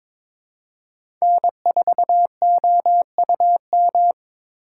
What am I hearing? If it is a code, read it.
Morse code, heard as a keyed tone, about 23 wpm, 700 Hz. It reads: N4OUM
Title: USA Call Signs - 5 Characters - 3x Character Spacing - 22wpm